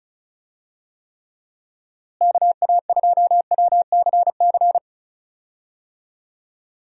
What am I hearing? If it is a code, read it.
Morse code, heard as a keyed tone, about 36 wpm, 700 Hz. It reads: KA2WCC
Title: USA Call Signs - 6 Characters - Repeated 3x - 35wpm